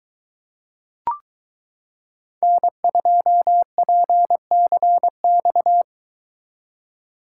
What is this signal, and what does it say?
Morse code, heard as a keyed tone, about 23 wpm, 700 Hz. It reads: N2PCX